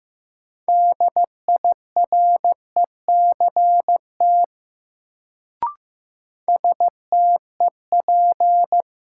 Morse code, heard as a keyed tone, about 15 wpm, 700 Hz. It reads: DIRECT STEP
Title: Sentences from Top 400 Words - New Word Drill - 15wpm